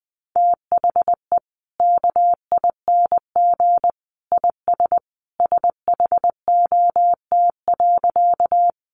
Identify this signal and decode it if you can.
Morse code, heard as a keyed tone, about 20 wpm, 700 Hz. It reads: THE KING IS SHOT.
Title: Sentences from Top 1100 Words 20wpm